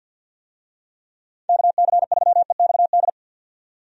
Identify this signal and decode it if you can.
Morse code, heard as a keyed tone, about 49 wpm, 700 Hz. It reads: KC3EXD
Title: USA Call Signs - 6 Characters - 50wpm